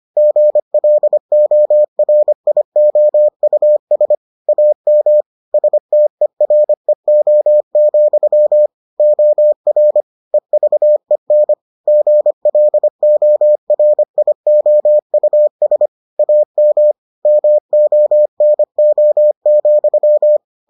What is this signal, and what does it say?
Morse code, heard as a keyed tone, about 25 wpm, 600 Hz. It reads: GLORIOUS AM STEREO, OR EVEN GLORIOUS AM MONO,